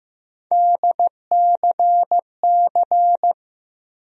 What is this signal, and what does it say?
Morse code, heard as a keyed tone, about 15 wpm, 700 Hz. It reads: DCC
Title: USA Call Sign Postfixes - 3 Letters 15wpm